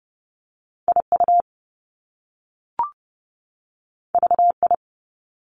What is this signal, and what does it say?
Morse code, heard as a keyed tone, about 30 wpm, 700 Hz. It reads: IU VI
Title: Top Two Letter Combinations in Words 30wpm